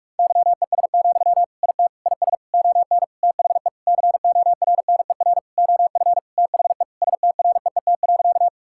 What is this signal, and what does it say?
Morse code, heard as a keyed tone, about 46 wpm, 700 Hz. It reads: YES, IT IS ON THE CORNER OF THE STREET.
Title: Sentences from Top 1600 Words 45wpm